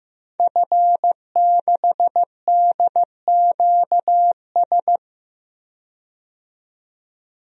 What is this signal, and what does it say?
Morse code, heard as a keyed tone, about 15 wpm, 700 Hz. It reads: F6DQS
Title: DXCC-100 Call Signs - 5 Characters - 15wpm